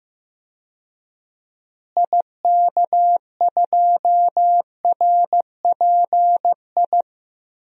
Morse code, heard as a keyed tone, about 15 wpm, 700 Hz. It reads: IK2RPI
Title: DXCC-100 Call Signs - 6 Characters - Repeated 2x - 15wpm